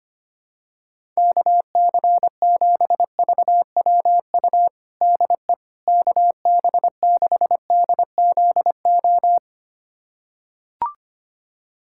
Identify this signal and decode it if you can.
Morse code, heard as a keyed tone, about 25 wpm, 700 Hz. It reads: KC7VWU DE KB6DZO
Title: USA Callsign DE Callsign 25wpm